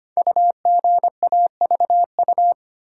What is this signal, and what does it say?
Morse code, heard as a keyed tone, about 25 wpm, 700 Hz. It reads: UGAVU
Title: Code Groups - 5 Letter:Numbers - 25wpm